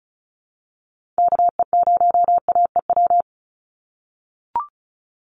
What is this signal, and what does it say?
Morse code, heard as a keyed tone, about 34 wpm, 700 Hz. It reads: KE0AEW
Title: USA Call Signs 35wpm